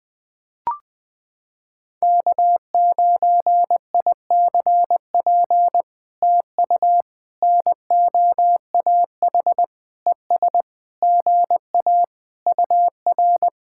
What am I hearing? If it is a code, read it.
Morse code, heard as a keyed tone, about 20 wpm, 700 Hz. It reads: K9ICP TU NOAH ES GA UR
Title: POTA QSO Practice - 20wpm